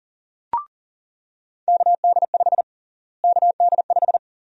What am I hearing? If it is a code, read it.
Morse code, heard as a keyed tone, about 40 wpm, 700 Hz. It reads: KD5 KD5
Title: USA Call Sign Prefixes - Repeated 2x - 40wpm